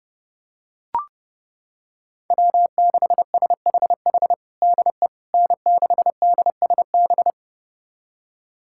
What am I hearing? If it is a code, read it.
Morse code, heard as a keyed tone, about 30 wpm, 700 Hz. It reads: W6SHH DE N6DSB
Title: USA Callsign DE Callsign - 5 Characters - 30wpm